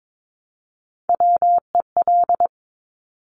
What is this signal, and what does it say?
Morse code, heard as a keyed tone, about 22 wpm, 700 Hz. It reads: WEL
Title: Top Three Letter Combinations in Words 22wpm